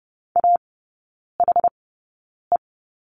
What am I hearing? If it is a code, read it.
Morse code, heard as a keyed tone, about 30 wpm, spaced as for 9 wpm, 700 Hz. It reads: AHE
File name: 081 Lesson 14e - Number 2 - Three Characters 30wpm